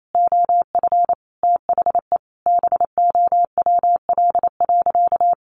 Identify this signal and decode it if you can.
Morse code, heard as a keyed tone, about 28 wpm, 700 Hz. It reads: OF THE BOWL.
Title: Sentences from Top 1200 Words 28wpm